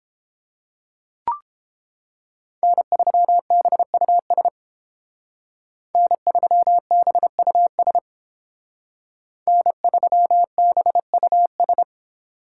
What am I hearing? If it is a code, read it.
Morse code, heard as a keyed tone, about 30 wpm, 700 Hz. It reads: N3BUS N3BUS N3BUS